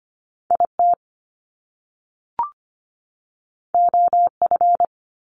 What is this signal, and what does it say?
Morse code, heard as a keyed tone, about 25 wpm, 700 Hz. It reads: IT OF
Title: Top Two Letter Combinations in Words 25wpm